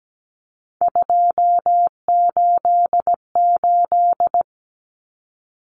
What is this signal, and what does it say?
Morse code, heard as a keyed tone, about 17 wpm, 700 Hz. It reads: 288
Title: Common Cut Numbers - 3-Digit 17wpm